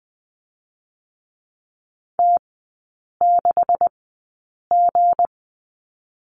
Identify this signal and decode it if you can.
Morse code, heard as a keyed tone, about 20 wpm, spaced as for 8 wpm, 700 Hz. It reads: T6G
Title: 212 Lesson 34e - Number 8 - Three Characters 20wpm